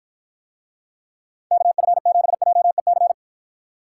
Morse code, heard as a keyed tone, about 52 wpm, 700 Hz. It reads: KF7JEC